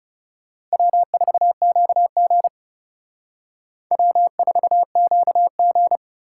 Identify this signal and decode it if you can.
Morse code, heard as a keyed tone, about 34 wpm, 700 Hz. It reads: W4QG W4QG
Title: Speed Racing - USA 1x2 Call Signs 30wpm